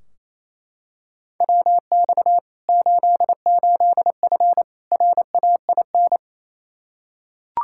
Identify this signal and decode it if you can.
Morse code, heard as a keyed tone, about 28 wpm, 700 Hz. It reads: WX 88F RAIN
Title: Practice QSO - Exchange 2 - Info - 28wpm